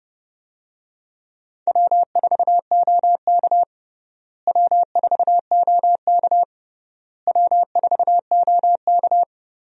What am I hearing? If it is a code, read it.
Morse code, heard as a keyed tone, about 30 wpm, 700 Hz. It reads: W4OK W4OK W4OK
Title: USA Call Signs - 4 Characters - Repeated 3x - 30wpm